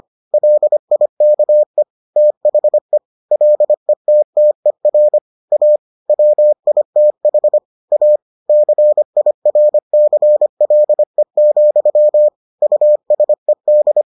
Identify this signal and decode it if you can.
Morse code, heard as a keyed tone, about 25 wpm, 600 Hz. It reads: LIKE THE LETTER A WITH A CIRCLE, USED